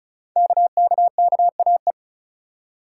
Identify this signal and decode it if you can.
Morse code, heard as a keyed tone, about 36 wpm, 700 Hz. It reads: KKKAE
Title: Code Groups - 5 Letters - 35wpm